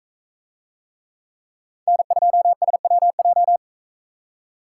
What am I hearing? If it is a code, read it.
Morse code, heard as a keyed tone, about 42 wpm, 700 Hz. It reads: N2SWJ